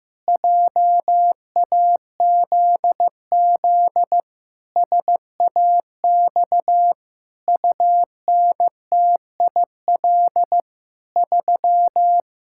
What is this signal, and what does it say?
Morse code, heard as a keyed tone, about 16 wpm, 700 Hz. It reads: JAZZ SAX UNTIL 3
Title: Pangram Challenge v2 - 15wpm